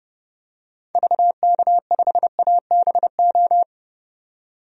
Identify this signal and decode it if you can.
Morse code, heard as a keyed tone, about 30 wpm, 700 Hz. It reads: VK5ABO
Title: DXCC-100 Call Signs - 6 Characters - 30wpm